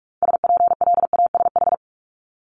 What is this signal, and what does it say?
Morse code, heard as a keyed tone, about 46 wpm, 700 Hz. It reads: SPLASH
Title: Multi-letter Phonemes Repeated 3x 45wpm